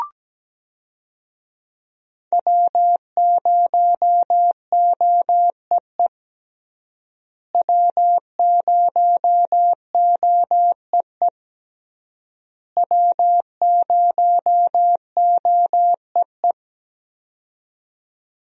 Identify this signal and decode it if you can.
Morse code, heard as a keyed tone, about 17 wpm, 700 Hz. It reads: W0OEE W0OEE W0OEE
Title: USA Call Signs - 5 Characters - Repeated 3x - 17wpm